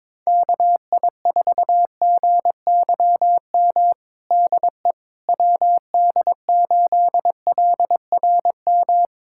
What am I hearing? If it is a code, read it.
Morse code, heard as a keyed tone, about 22 wpm, 700 Hz. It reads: KI4GYM DE WD8LRM
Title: USA Callsign DE Callsign - 6 Characters - 22wpm